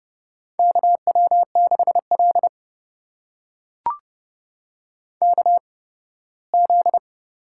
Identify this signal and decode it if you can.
Morse code, heard as a keyed tone, about 30 wpm, spaced as for 18 wpm, 700 Hz. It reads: KW6L KZ
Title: USA Call Signs - 4 Characters - 8x Character Spacing - 30wpm